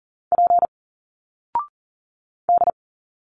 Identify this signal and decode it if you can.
Morse code, heard as a keyed tone, about 40 wpm, 700 Hz. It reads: P D
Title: Single Letters 40wpm